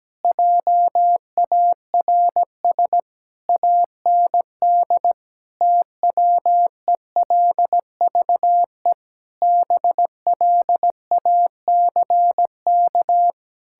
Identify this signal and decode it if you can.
Morse code, heard as a keyed tone, about 17 wpm, 700 Hz. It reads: JARS AND TWELVE BLACK
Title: Pangram Challenge v2 - 17wpm